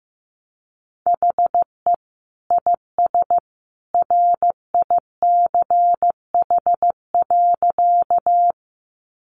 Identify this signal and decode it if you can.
Morse code, heard as a keyed tone, about 14 wpm, 700 Hz. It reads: HE IS RICH.
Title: Sentences from Top 1100 Words 15wpm